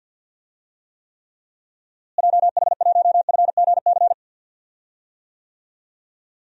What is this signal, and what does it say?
Morse code, heard as a keyed tone, about 52 wpm, 700 Hz. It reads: JH1FGC
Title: DXCC-100 Call Signs - 6 Characters - Repeated 3x - 50wpm